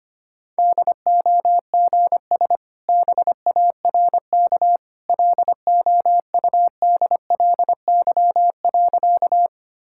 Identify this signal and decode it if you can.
Morse code, heard as a keyed tone, about 25 wpm, 690 Hz. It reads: DOGS BARK LOUDLY.
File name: Easy Sentences with Hints - 25wpm